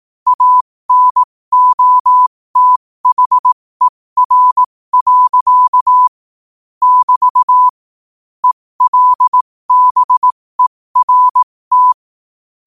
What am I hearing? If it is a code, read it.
Morse code, heard as a keyed tone, about 18 wpm, spaced as for 15 wpm, 1000 Hz. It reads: ANOTHER. <BT> ELBERT
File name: qotdfiles / 15wpm